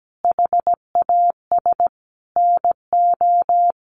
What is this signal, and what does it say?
Morse code, heard as a keyed tone, about 17 wpm, 700 Hz. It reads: HAS NO
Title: Sentences from Top 1100 Words 17wpm